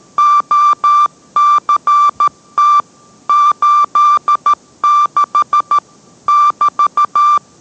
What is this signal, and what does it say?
Morse code, heard as a keyed tone, about 13 wpm, 1180 Hz. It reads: OCT 86 <BT>